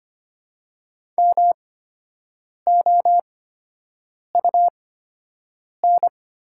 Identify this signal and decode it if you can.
Morse code, heard as a keyed tone, about 25 wpm, spaced as for 7 wpm, 700 Hz. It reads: MOUN
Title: 8x Character Spacing - Top 300 Words - 25wpm